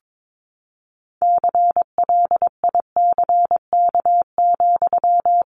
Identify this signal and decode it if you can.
Morse code, heard as a keyed tone, about 22 wpm, 700 Hz. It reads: CLICK,
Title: Sentences from Top 1100 Words 22wpm